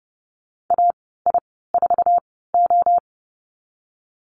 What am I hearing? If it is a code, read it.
Morse code, heard as a keyed tone, about 30 wpm, spaced as for 17 wpm, 700 Hz. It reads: AI4O
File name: Lesson 14g - Number 2 - Call Signs 30wpm